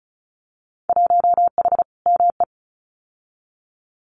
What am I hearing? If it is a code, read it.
Morse code, heard as a keyed tone, about 34 wpm, 700 Hz. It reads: 1H ME
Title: Winter Field Day Exchanges 35wpm